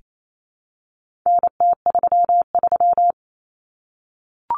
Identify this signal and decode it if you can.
Morse code, heard as a keyed tone, about 28 wpm, 700 Hz. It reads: NT33
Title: Common Cut Numbers - 4-Digit 28wpm